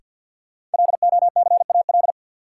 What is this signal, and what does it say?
Morse code, heard as a keyed tone, about 52 wpm, 700 Hz. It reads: LOCAL
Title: Sentences from Top 1000 Words - Review Part 3 - New Word Drill - 50wpm